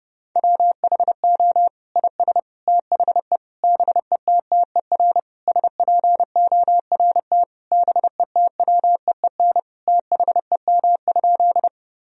Morse code, heard as a keyed tone, about 32 wpm, 700 Hz. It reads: WHO IS THE BETTER SPORT BETWEEN THEM?
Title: Sentences from Top 1000 Words - Review Part 4 - 30wpm